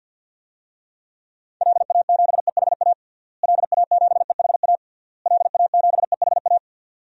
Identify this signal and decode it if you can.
Morse code, heard as a keyed tone, about 53 wpm, 700 Hz. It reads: LA7EHA LA7EHA LA7EHA
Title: DXCC-100 Call Signs - 6 Characters - Repeated 3x - 50wpm